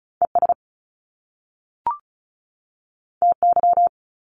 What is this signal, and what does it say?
Morse code, heard as a keyed tone, about 34 wpm, 700 Hz. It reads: ES TY
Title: Top Two Letter Combinations in Words 35wpm